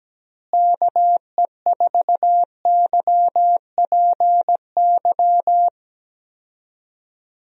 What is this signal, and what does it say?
Morse code, heard as a keyed tone, about 17 wpm, 700 Hz. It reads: KE4YPY
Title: USA Call Signs - 6 Characters - Repeated 3x - 17wpm